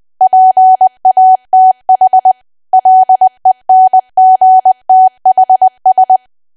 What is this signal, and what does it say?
Morse code, heard as a keyed tone, about 20 wpm, 750 Hz. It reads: PATH LENGTHS
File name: cwa / W1AW-Level3 / 20WPM